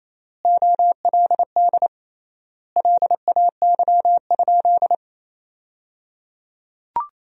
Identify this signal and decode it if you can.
Morse code, heard as a keyed tone, about 28 wpm, 700 Hz. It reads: OLD LAY?
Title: 3x Word-Spacing - Sentences from Top 400 Words - 28wpm